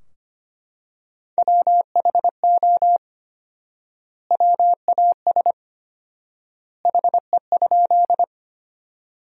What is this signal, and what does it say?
Morse code, heard as a keyed tone, about 25 wpm, 700 Hz. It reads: WHO WAS HE?
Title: 4x Word-Spacing - Sentences from Top 100 Words - 25wpm